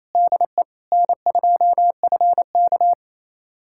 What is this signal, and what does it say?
Morse code, heard as a keyed tone, about 28 wpm, 700 Hz. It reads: DE N2FK
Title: USA Callsign DE Callsign - 4 Characters - 28wpm